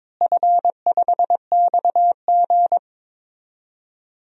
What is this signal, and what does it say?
Morse code, heard as a keyed tone, about 22 wpm, 700 Hz. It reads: F5XG
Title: DXCC-100 Call Signs - 4 Characters - Repeated 2x - 22wpm